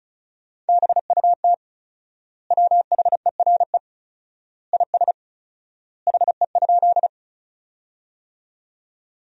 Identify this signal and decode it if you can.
Morse code, heard as a keyed tone, about 36 wpm, 700 Hz. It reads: BUT WHERE IS HE?
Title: 4x Word-Spacing - Sentences from Top 200 Words - 35wpm